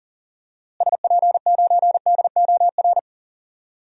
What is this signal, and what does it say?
Morse code, heard as a keyed tone, about 40 wpm, 700 Hz. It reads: SP9DOR
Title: DXCC-100 Call Signs - 6 Characters - Repeated 3x - 40wpm